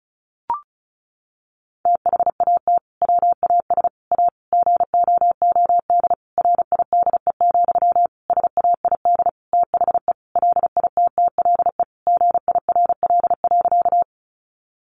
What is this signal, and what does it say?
Morse code, heard as a keyed tone, about 34 wpm, 700 Hz. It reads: THAT WAS A GOOD RIDE, SAID THE LITTLE GIRL.
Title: Sentences from Top 1200 Words 35wpm